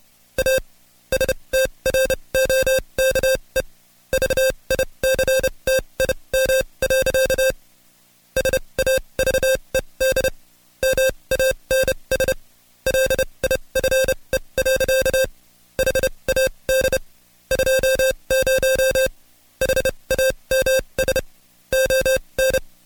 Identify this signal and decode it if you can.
Morse code, heard as a keyed tone, about 29 wpm, spaced as for 23 wpm, 530 Hz. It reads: A STROKE VICTIM. SAVED MANS LIFE. HAD 20 HAMS ON
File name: Code Quick / AudioDownld2 / Seg138-Stepper11 23WPM